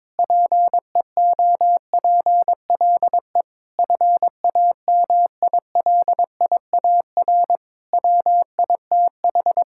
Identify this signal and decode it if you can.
Morse code, heard as a keyed tone, about 22 wpm, 700 Hz. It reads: PEOPLE FAMILIAR WITH
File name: Sentences with 7-Word Phrases - 22wpm